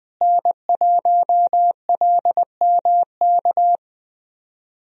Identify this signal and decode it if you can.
Morse code, heard as a keyed tone, about 20 wpm, 700 Hz. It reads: N1LMK
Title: USA Call Signs - 5 Characters - Repeated 2x - 20wpm